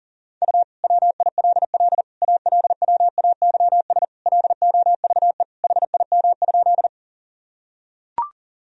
Morse code, heard as a keyed tone, about 37 wpm, 700 Hz. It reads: U WILL ALWAYS LOVE HIM?
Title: Sentences from Top 400 Words and Abbreviations 40wpm